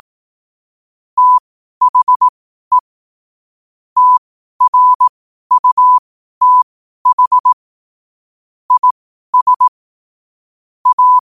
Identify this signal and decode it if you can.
Morse code, heard as a keyed tone, about 18 wpm, spaced as for 12 wpm, 1000 Hz. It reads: THE TRUTH IS A
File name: qotdfiles / 12wpm